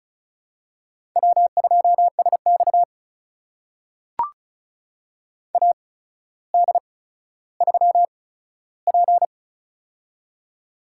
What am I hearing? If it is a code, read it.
Morse code, heard as a keyed tone, about 34 wpm, spaced as for 15 wpm, 700 Hz. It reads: W2SX AD3P
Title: USA Call Signs - 4 Characters - 8x Character Spacing - 35wpm